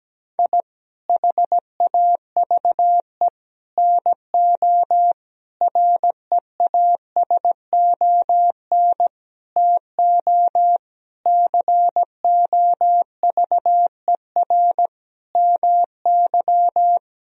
Morse code, heard as a keyed tone, about 17 wpm, 700 Hz. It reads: I HAVE NO REASON TO COVER MY
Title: Sentences from Top 1000 Words - Review Part 1 - 17wpm